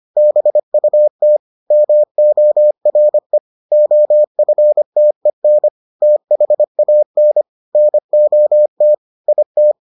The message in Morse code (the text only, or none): BUT MORE OFTEN THAN NOT IT